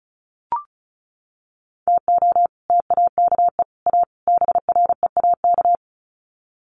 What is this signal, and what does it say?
Morse code, heard as a keyed tone, about 34 wpm, 700 Hz. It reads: TO TAKE A BREAK
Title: Sets of 4 Words 35wpm